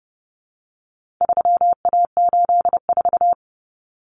Morse code, heard as a keyed tone, about 30 wpm, 700 Hz. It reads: 3A84
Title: Common Cut Numbers - 4-Digit 30wpm